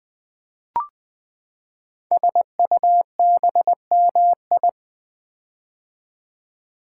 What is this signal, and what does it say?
Morse code, heard as a keyed tone, about 20 wpm, 700 Hz. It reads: SUBMI
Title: Build A Word - 5 Letters At A Time - 10-Letter Words - 20wpm